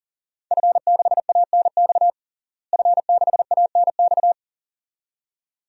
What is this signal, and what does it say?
Morse code, heard as a keyed tone, about 37 wpm, 700 Hz. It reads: F6ANX F6ANX
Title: DXCC-100 Call Signs - 5 Characters - Repeated 2x - 40wpm